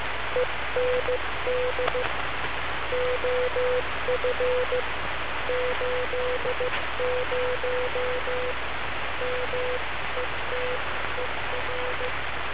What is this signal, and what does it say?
Morse code, heard as a keyed tone, about 15 wpm, 500 Hz, with noise. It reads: END OF 80 METER
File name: wa6zty-fall-80m-snip